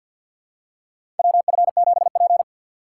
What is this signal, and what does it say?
Morse code, heard as a keyed tone, about 50 wpm, 690 Hz. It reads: WF7P
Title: USA Call Signs - 6x Character Spacing - 50wpm